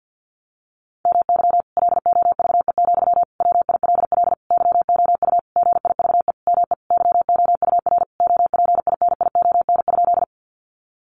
Morse code, heard as a keyed tone, about 50 wpm, 700 Hz. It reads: MY LOVE, WILL YOU GIVE ME YOUR OPINION?
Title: Sentences from Top 1200 Words 50wpm